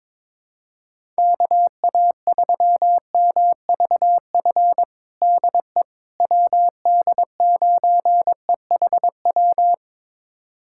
Morse code, heard as a keyed tone, about 22 wpm, 700 Hz. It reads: KA3MVF DE WD9EHW
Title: USA Callsign DE Callsign 22wpm